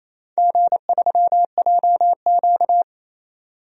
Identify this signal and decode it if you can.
Morse code, heard as a keyed tone, about 27 wpm, 700 Hz. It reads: G3JQ
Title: DXCC-100 Call Signs - 4 Characters - 28wpm